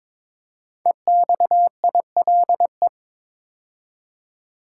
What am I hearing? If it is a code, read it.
Morse code, heard as a keyed tone, about 22 wpm, 700 Hz. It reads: EXILE
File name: 5 Letter Words - First 3 Letters Given - 22wpm